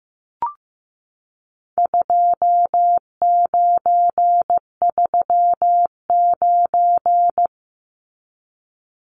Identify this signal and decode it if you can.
Morse code, heard as a keyed tone, about 15 wpm, 700 Hz. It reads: 2939
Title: Numbers - 4-Digit 15wpm